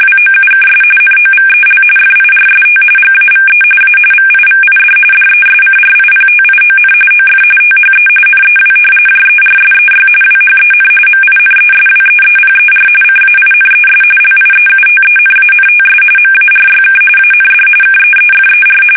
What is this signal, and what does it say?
RTTY, about 45.45 baud. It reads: WOUKS NOVEL FIRST GAVE A VIVID DESCRIPTION OF THE NEW JERSEY WHEN MINESWEEPER CAINES LIEUTENANTS MARYK AND KEEFER
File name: fsk850 / battleshipnj